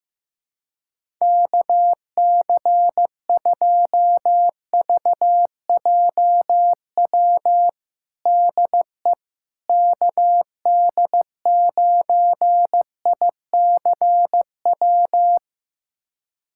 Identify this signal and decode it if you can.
Morse code, heard as a keyed tone, about 15 wpm, 700 Hz. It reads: KC2VJW DE KD9ICW
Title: USA Callsign DE Callsign 15wpm